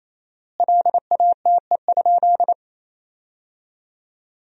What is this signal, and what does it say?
Morse code, heard as a keyed tone, about 28 wpm, 700 Hz. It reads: LATE?
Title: 4x Word-Spacing - Sentences from Top 300 Words - 28wpm